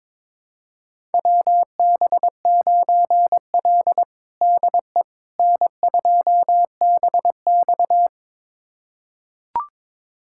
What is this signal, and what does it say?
Morse code, heard as a keyed tone, about 22 wpm, 700 Hz. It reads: WB9L DE N2BX
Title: USA Callsign DE Callsign - 4 Characters - 22wpm